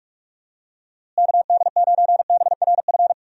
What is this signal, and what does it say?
Morse code, heard as a keyed tone, about 44 wpm, 700 Hz. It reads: KD9BRF
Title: USA Call Signs - 6 Characters - Repeated 2x - 45wpm